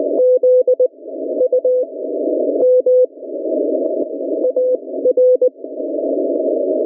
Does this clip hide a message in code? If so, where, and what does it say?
Yes: Morse code from 0.2 to 5.5 s, heard as a keyed tone, about 20 wpm, spaced as for 16 wpm, 500 Hz. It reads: Z U M AR